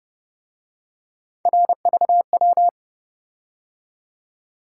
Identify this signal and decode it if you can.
Morse code, heard as a keyed tone, about 30 wpm, 700 Hz. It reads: RVW
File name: Speed-Racing - Three Letter-Number Combinations - 25wpm